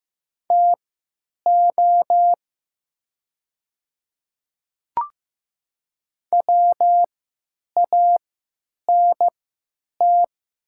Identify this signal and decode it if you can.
Morse code, heard as a keyed tone, about 15 wpm, spaced as for 9 wpm, 700 Hz. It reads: TO WANT
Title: 3x Character Spacing - Top 200 Words - 15wpm